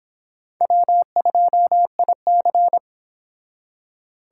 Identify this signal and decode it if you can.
Morse code, heard as a keyed tone, about 30 wpm, 700 Hz. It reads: W2IC